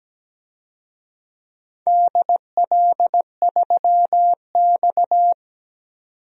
Morse code, heard as a keyed tone, about 17 wpm, 700 Hz. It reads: DL3X
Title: DXCC-100 Call Signs - 4 Characters - 17wpm